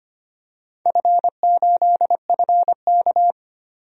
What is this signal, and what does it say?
Morse code, heard as a keyed tone, about 25 wpm, 700 Hz. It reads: F8FK
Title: DXCC-100 Call Signs - 4 Characters - 25wpm